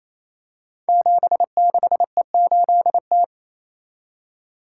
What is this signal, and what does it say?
Morse code, heard as a keyed tone, about 28 wpm, 690 Hz. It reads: 76E8T
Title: Code Groups - 5 Letter:Numbers - 28wpm